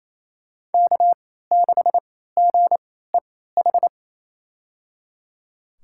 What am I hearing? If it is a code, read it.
Morse code, heard as a keyed tone, about 28 wpm, spaced as for 16 wpm, 700 Hz. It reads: K6GEH
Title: USA Call Signs - 3x Character Spacing - 28wpm